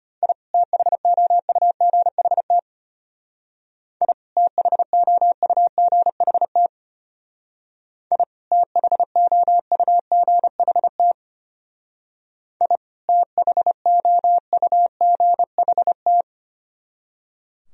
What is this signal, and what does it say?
Morse code, heard as a keyed tone, about 33 wpm, 700 Hz. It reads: I THOUGHT I THOUGHT I THOUGHT I THOUGHT